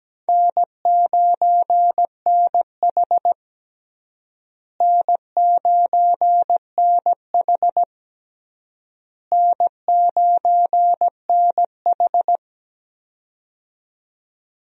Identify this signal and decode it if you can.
Morse code, heard as a keyed tone, about 17 wpm, 700 Hz. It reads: N9NH N9NH N9NH
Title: USA Call Signs - 4 Characters - Repeated 3x - 17wpm